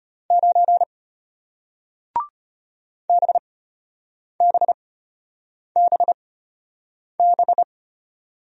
Morse code, heard as a keyed tone, about 33 wpm, spaced as for 8 wpm, 700 Hz. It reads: 9 BBBB